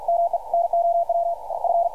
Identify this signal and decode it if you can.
Morse code, heard as a keyed tone, about 52 wpm, 680 Hz, with radio noise. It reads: ME NOW F
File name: archived / CRSnet